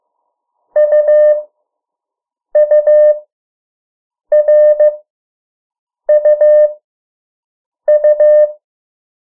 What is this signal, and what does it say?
Morse code, heard as a keyed tone, about 15 wpm, spaced as for 6 wpm, 600 Hz. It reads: UURUU
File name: morse mp3 / 03 urok 2